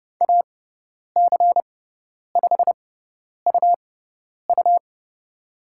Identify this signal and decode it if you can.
Morse code, heard as a keyed tone, about 30 wpm, spaced as for 10 wpm, 700 Hz. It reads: AC5UU
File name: Lesson 18g - Letter m - Call Signs 30wpm